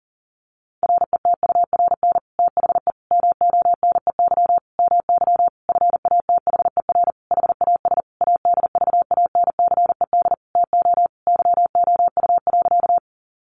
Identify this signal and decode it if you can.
Morse code, heard as a keyed tone, about 40 wpm, 700 Hz. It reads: RETURN THE MONEY MY FATHER HAS ADVANCED TO YOU.
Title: Sentences from Top 1200 Words 40wpm